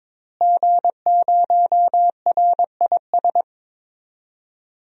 Morse code, heard as a keyed tone, about 21 wpm, 700 Hz. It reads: G0RIS
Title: DXCC-100 Call Signs - 5 Characters - Repeated 3x - 22wpm